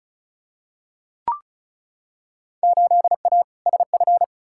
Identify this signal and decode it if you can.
Morse code, heard as a keyed tone, about 36 wpm, 700 Hz. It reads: 8A SF
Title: Field Day Exchanges Encore - 35wpm